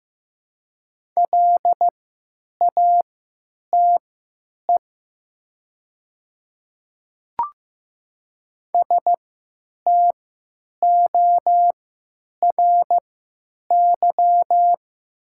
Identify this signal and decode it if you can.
Morse code, heard as a keyed tone, about 15 wpm, spaced as for 8 wpm, 700 Hz. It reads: LATE STORY
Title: 3x Character Spacing - Top 300 Words - 15wpm